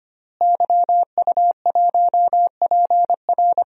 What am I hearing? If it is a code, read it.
Morse code, heard as a keyed tone, about 25 wpm, 700 Hz. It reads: YU1PR
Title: DXCC-100 Call Signs - 5 Characters - Repeated 2x - 25wpm